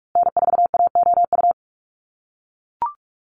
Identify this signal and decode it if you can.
Morse code, heard as a keyed tone, about 44 wpm, 700 Hz. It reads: N4AOU
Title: USA Call Signs Repeated 2x 45wpm